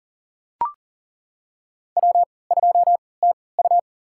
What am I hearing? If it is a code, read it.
Morse code, heard as a keyed tone, about 41 wpm, spaced as for 23 wpm, 700 Hz. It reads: W2TU
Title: USA Call Signs - 4 Characters - 3x Character Spacing - 40wpm